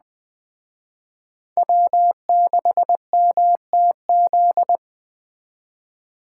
Morse code, heard as a keyed tone, about 20 wpm, 700 Hz. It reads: W6MTZ
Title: Speed Racing - USA 1x3 Call Signs 17wpm